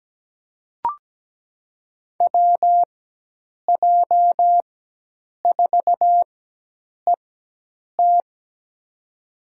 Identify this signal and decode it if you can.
Morse code, heard as a keyed tone, about 17 wpm, spaced as for 8 wpm, 700 Hz. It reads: WJ4ET
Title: USA Call Signs - 5 Characters - 4x Character Spacing - 17wpm